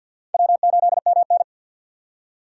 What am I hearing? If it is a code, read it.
Morse code, heard as a keyed tone, about 52 wpm, 700 Hz. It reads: W8MN
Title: USA Call Signs - 4 Characters - 6x Character Spacing - 50wpm